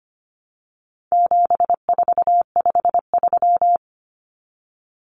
Morse code, heard as a keyed tone, about 25 wpm, 700 Hz. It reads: 7453
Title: Numbers - 4-Digit 25wpm